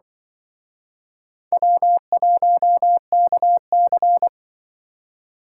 Morse code, heard as a keyed tone, about 25 wpm, 700 Hz. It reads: W1KC